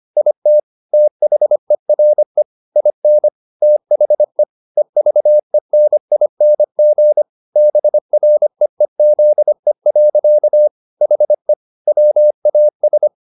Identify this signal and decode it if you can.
Morse code, heard as a keyed tone, about 25 wpm, 600 Hz. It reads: IT THERE IN THE EVENING BREEZE. HE WAS
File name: the old man and the sea part 1 25wpm